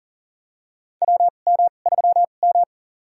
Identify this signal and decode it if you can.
Morse code, heard as a keyed tone, about 40 wpm, spaced as for 29 wpm, 700 Hz. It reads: WM3M
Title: USA Call Signs - 2x Character Spacing - 40wpm